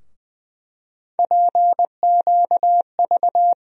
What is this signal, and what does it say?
Morse code, heard as a keyed tone, about 21 wpm, 700 Hz. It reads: PQV
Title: USA Call Sign Postfixes - 3 Letters 20wpm